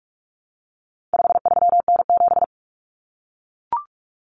Morse code, heard as a keyed tone, about 44 wpm, 700 Hz. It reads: 53N7
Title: Common Cut Numbers - 4-Digit 45wpm